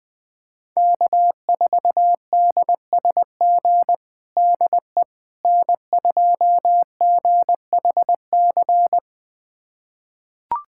Morse code, heard as a keyed tone, about 21 wpm, 700 Hz. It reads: K4DSG DE N2GHC
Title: USA Callsign DE Callsign - 5 Characters - 20wpm